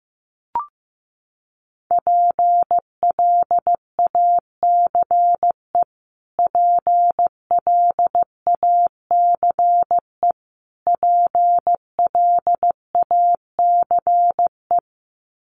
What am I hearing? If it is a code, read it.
Morse code, heard as a keyed tone, about 15 wpm, 700 Hz. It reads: PLACE PLACE PLACE
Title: Top 200 Words Repeated 15wpm